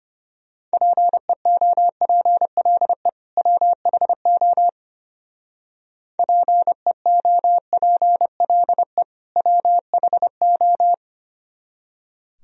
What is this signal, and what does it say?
Morse code, heard as a keyed tone, about 30 wpm, 700 Hz. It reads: PEOPLE WHO PEOPLE WHO
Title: Speed-Racing - Sets of 2 Words 25wpm